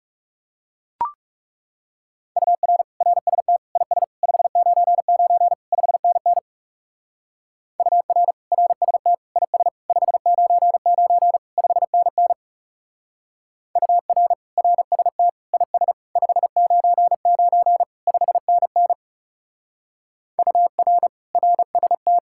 Morse code, heard as a keyed tone, about 34 wpm, 700 Hz. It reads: UR RST IS 599 5NN UR RST IS 599 5NN UR RST IS 599 5NN UR RST
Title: Speed-Racing Common QSO Phrases - 30wpm